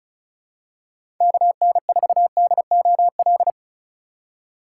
Morse code, heard as a keyed tone, about 34 wpm, 700 Hz. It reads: KN4DOL
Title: USA Call Signs - 6x Character Spacing - 35wpm